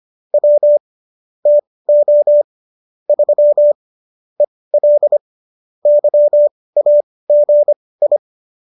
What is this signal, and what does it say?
Morse code, heard as a keyed tone, about 25 wpm, spaced as for 18 wpm, 590 Hz. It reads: W TO 3 EL YAGI